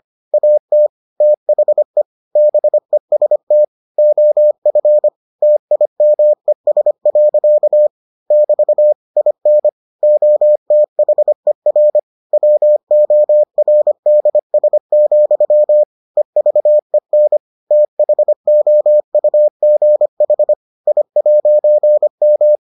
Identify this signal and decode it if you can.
Morse code, heard as a keyed tone, about 25 wpm, 600 Hz. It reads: AT THE BEST OF TIMES. <BT> IN OTHER WORDS, EVEN THOUGH I'M